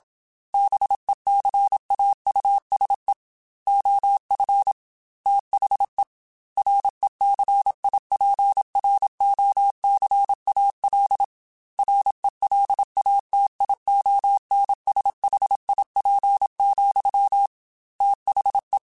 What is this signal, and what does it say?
Morse code, heard as a keyed tone, about 27 wpm, 800 Hz. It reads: BECAUSE OF THE RECIPROCAL RELATIONSHIP, THE
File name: Technical Subjects / Technical Subjects-mp3s / XC @25